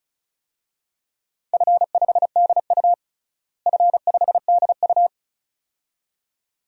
Morse code, heard as a keyed tone, about 34 wpm, 700 Hz. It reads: F5DU F5DU
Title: DXCC-100 Call Signs - 4 Characters - Repeated 2x - 35wpm